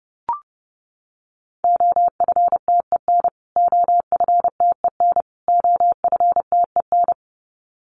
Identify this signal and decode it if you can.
Morse code, heard as a keyed tone, about 30 wpm, 700 Hz. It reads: OFTEN OFTEN OFTEN
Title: Top 300 Words Repeated 30wpm